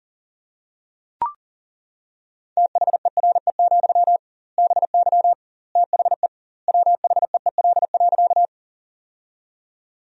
Morse code, heard as a keyed tone, about 37 wpm, 700 Hz. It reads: THERE, BY THE WHEEL.
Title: 2x Word-Spacing - Sentences from Top 500 Words - 40wpm